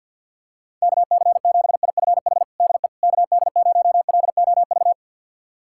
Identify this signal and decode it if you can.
Morse code, heard as a keyed tone, about 51 wpm, 690 Hz. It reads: KC7IFH DE KD0LOV
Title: USA Callsign DE Callsign 50wpm